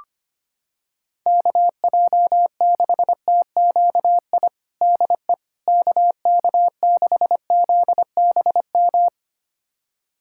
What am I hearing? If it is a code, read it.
Morse code, heard as a keyed tone, about 25 wpm, 700 Hz. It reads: KJ6TQI DE KK6ZBM
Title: USA Callsign DE Callsign 25wpm